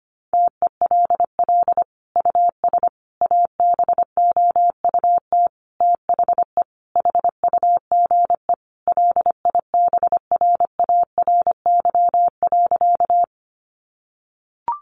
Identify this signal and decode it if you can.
Morse code, heard as a keyed tone, about 25 wpm, 700 Hz. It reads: TELL US ABOUT THE HUGE LIBRARY.
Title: Sentences from Top 1100 Words 25wpm